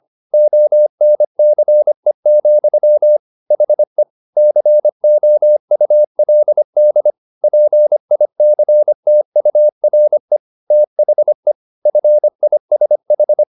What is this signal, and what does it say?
Morse code, heard as a keyed tone, about 25 wpm, 600 Hz. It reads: ONCE, HE COULD PICTURE THE FISH
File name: the old man and the sea part 1 25wpm